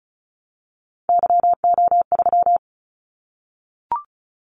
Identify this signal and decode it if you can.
Morse code, heard as a keyed tone, about 36 wpm, 700 Hz. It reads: YO3
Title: Callsign Prefixes for Top 25 DXCC - 35wpm